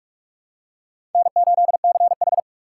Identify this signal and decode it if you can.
Morse code, heard as a keyed tone, about 46 wpm, 700 Hz. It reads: N8CH
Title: USA Call Signs - 3x Character Spacing - 45wpm